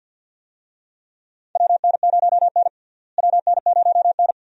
Morse code, heard as a keyed tone, about 48 wpm, 700 Hz. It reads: WN0N WN0N
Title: USA Call Signs - 4 Characters - Repeated 2x - 50wpm